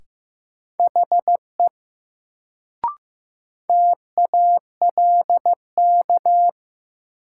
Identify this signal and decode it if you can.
Morse code, heard as a keyed tone, about 15 wpm, 700 Hz. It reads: HE TALK
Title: Sets of 2 Words - New Word Drill - 15wpm